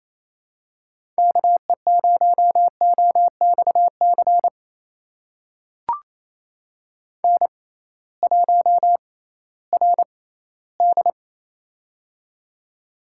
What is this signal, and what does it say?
Morse code, heard as a keyed tone, about 28 wpm, spaced as for 16 wpm, 700 Hz. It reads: KE0OXC N1RD
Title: USA Call Signs - 6x Character Spacing - 28wpm